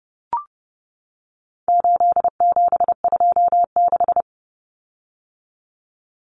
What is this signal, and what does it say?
Morse code, heard as a keyed tone, about 30 wpm, 700 Hz. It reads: 8726
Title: Common Cut Numbers - 4-Digit 30wpm